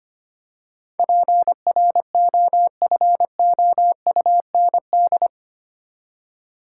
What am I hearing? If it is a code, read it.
Morse code, heard as a keyed tone, about 25 wpm, 700 Hz. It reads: PROFOUND
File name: 8 Letter Words - First 3 Letters Given - 25wpm